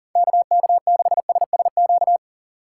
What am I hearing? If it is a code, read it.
Morse code, heard as a keyed tone, about 38 wpm, 700 Hz. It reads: KK6SSQ
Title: USA Call Signs - 6 Characters - 3x Character Spacing - 40wpm